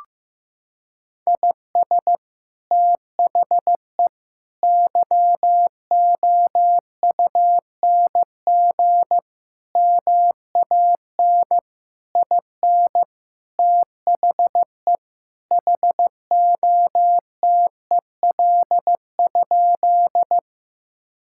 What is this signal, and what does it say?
Morse code, heard as a keyed tone, about 15 wpm, 700 Hz. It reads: IS THE YOUNG MAN IN THE HOTEL?
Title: Sentences from Top 1600 Words 15wpm